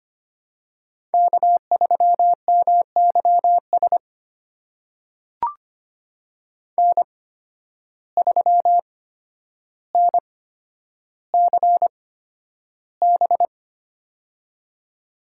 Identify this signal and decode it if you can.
Morse code, heard as a keyed tone, about 25 wpm, spaced as for 11 wpm, 700 Hz. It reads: K3MYS N3NCB
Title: USA Call Signs - 5 Characters - 8x Character Spacing - 25wpm